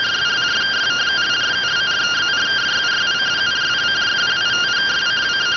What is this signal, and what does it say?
Two-tone teleprinter signal, about 170 Hz shift.